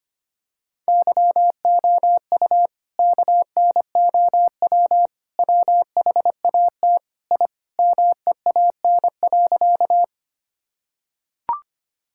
Morse code, heard as a keyed tone, about 25 wpm, 700 Hz. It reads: YOU KNOW WHAT I MEAN.
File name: practice / 25wpm / Sentences with 4-Word Phrases - 25wpm